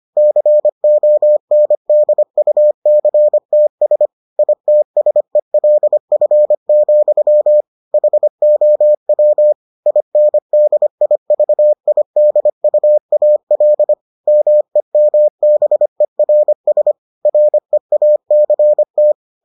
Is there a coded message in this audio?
Yes: Morse code from 0.2 to 19.1 s, heard as a keyed tone, about 25 wpm, 600 Hz. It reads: CONDUCTS ITSELF, HOW INDIVIDUAL MEMBERS REACT